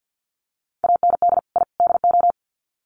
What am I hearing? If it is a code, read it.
Morse code, heard as a keyed tone, about 50 wpm, 700 Hz. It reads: AND I DO
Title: Sets of 3 Words 50wpm